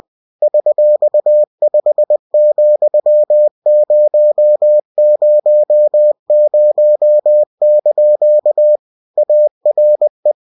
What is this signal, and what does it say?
Morse code, heard as a keyed tone, about 20 wpm, 600 Hz. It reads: $5,000) ARE
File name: cw-books / winnie-the-pooh / winnie-20wpm-page-0121